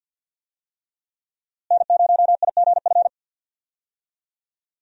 Morse code, heard as a keyed tone, about 52 wpm, 700 Hz. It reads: N0IGF
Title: USA Call Signs - 5 Characters - Repeated 3x - 50wpm